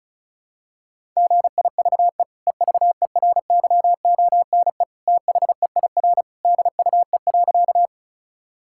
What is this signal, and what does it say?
Morse code, heard as a keyed tone, about 36 wpm, 700 Hz. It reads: GIVE EVERYONE THEIR DUE.
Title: Sentences from Top 1000 Words - Review Part 6 - 35wpm